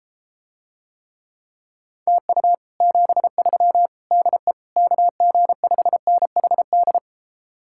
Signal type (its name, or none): Morse code